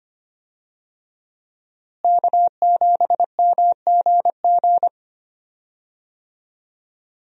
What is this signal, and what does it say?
Morse code, heard as a keyed tone, about 25 wpm, 700 Hz. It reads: K7MGG
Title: USA Call Signs - 5 Characters - Repeated 3x - 25wpm